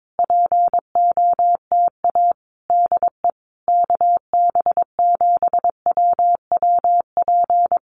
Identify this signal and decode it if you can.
Morse code, heard as a keyed tone, about 22 wpm, 700 Hz. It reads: POTA DE KB7WWP
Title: Common QSO Phrases Encore 22wpm